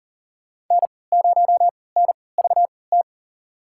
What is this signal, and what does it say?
Morse code, heard as a keyed tone, about 39 wpm, spaced as for 22 wpm, 700 Hz. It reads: N0NVT
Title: USA Call Signs - 5 Characters - 3x Character Spacing - 40wpm